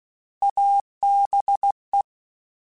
Morse code, heard as a keyed tone, about 16 wpm, 800 Hz. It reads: ABE
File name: Alphabetical / Alphabetical mp3s / As @15